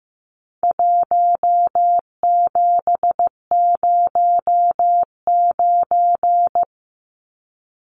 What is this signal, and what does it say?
Morse code, heard as a keyed tone, about 15 wpm, 700 Hz. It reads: 1709
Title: Numbers - Contest Style - 4-Digit 15wpm